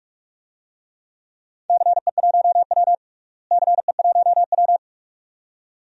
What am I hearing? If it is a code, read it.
Morse code, heard as a keyed tone, about 46 wpm, 700 Hz. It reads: CE1W CE1W
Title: DXCC-100 Call Signs - 4 Characters - Repeated 2x - 45wpm